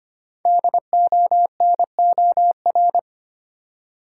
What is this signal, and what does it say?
Morse code, heard as a keyed tone, about 25 wpm, 700 Hz. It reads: DONOR
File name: 5 Letter Words - First 3 Letters Given - 25wpm